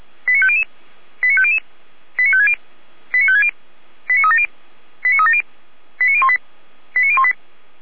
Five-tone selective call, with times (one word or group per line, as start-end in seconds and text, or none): 0.3-0.6 89500
1.2-1.6 89500
2.2-2.5 89560
3.1-3.5 89560
4.1-4.5 89370
5.0-5.4 89370
6.0-6.4 89018
7.0-7.3 89018